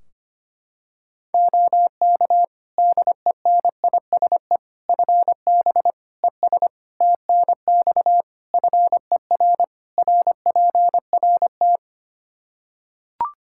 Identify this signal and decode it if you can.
Morse code, heard as a keyed tone, about 25 wpm, 700 Hz. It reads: OK DENISE FB ES TNX FER RPRT
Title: K1USN SST QSO Practice 25wpm